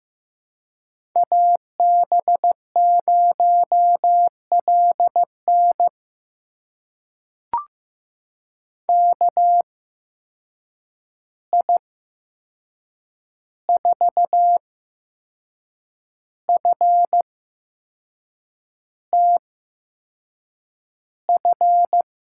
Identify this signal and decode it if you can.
Morse code, heard as a keyed tone, about 15 wpm, spaced as for 6 wpm, 700 Hz. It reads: AB0LN KI4FTF
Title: USA Call Signs - 8x Character Spacing - 15wpm